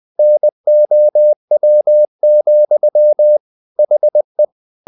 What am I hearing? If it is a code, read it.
Morse code, heard as a keyed tone, about 20 wpm, 600 Hz. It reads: NOW, HE
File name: the old man and the sea part 2 20wpm